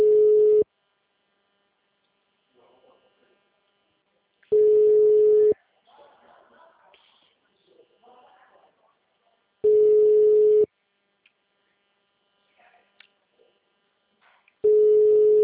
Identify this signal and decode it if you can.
Ringback tone (Europe, 425 Hz), with speech.
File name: out- -2002- - - .97958